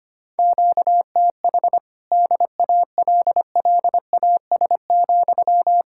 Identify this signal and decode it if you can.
Morse code, heard as a keyed tone, about 24 wpm, 700 Hz. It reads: QTH DALLAS,
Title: Speed-Racing Common QSO Phrases Encore - 25wpm